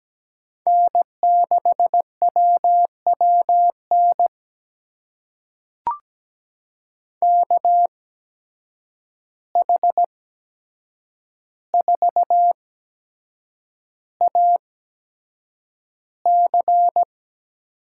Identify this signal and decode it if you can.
Morse code, heard as a keyed tone, about 17 wpm, spaced as for 7 wpm, 700 Hz. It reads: N6WWN KH4AC
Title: USA Call Signs - 5 Characters - 8x Character Spacing - 17wpm